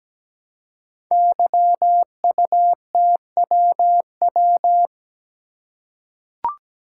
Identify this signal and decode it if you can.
Morse code, heard as a keyed tone, about 17 wpm, 700 Hz. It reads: YUTWW
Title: Code Groups - 5 Letters - 17wpm